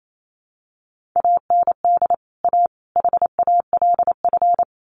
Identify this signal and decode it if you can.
Morse code, heard as a keyed tone, about 28 wpm, 700 Hz. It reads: AND A HALF
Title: Sets of 3 Words 28wpm